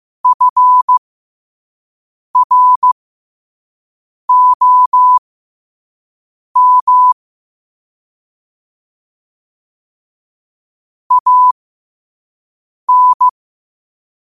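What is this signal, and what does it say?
Morse code, heard as a keyed tone, about 15 wpm, spaced as for 5 wpm, 1000 Hz. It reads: FROM AN
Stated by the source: Morse code quote of the day sent at 5 wpm.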